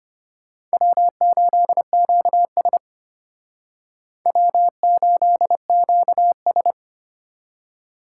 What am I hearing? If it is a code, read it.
Morse code, heard as a keyed tone, about 33 wpm, 700 Hz. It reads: W8QS W8QS